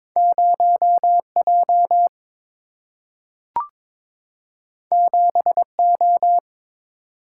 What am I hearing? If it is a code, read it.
Morse code, heard as a keyed tone, about 22 wpm, 700 Hz. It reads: 0 J 7 O
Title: Single-Letter Single-Number 22wpm